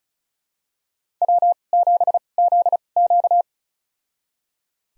Morse code, heard as a keyed tone, about 34 wpm, spaced as for 25 wpm, 700 Hz. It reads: W7ZQ
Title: USA Call Signs - 4 Characters - 2x Character Spacing - 35wpm